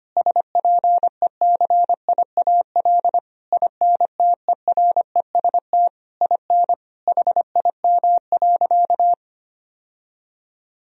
Morse code, heard as a keyed tone, about 25 wpm, 700 Hz. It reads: SPECIAL INTEREST IN HIM.
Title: Sentences from Top 1000 Words - Review Part 3 - 25wpm